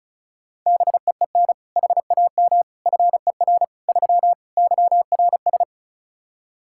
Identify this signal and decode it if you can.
Morse code, heard as a keyed tone, about 34 wpm, 700 Hz. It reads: BEEN HAM FER 3 YRS
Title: K1USN SST QSO Practice 35wpm